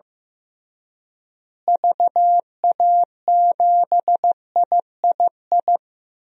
Morse code, heard as a keyed tone, about 15 wpm, 700 Hz. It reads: VA7III
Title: DXCC-100 Call Signs - 6 Characters - Repeated 3x - 15wpm